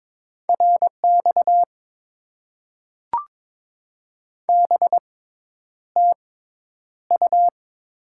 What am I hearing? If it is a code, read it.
Morse code, heard as a keyed tone, about 22 wpm, spaced as for 10 wpm, 700 Hz. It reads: RX BTU
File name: Top 90 QSO Elements - 6x Character Spacing - 22wpm